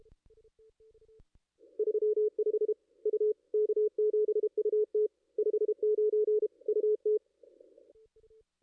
Morse code, heard as a keyed tone, about 31 wpm, 440 Hz. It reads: 35 U K7UT 59 UT
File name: contests / ss-cw-k5tr-op-n5rz -2200